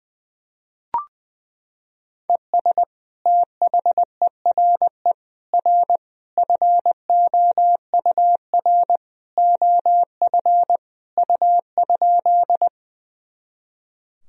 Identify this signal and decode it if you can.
Morse code, heard as a keyed tone, about 21 wpm, 700 Hz. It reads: ES THERE R FOUR OF U?
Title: Sentences from Top 300 Words and Abbreviations 20wpm